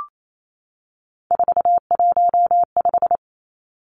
Morse code, heard as a keyed tone, about 28 wpm, 700 Hz. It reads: 415
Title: Numbers - 3-Digit 28wpm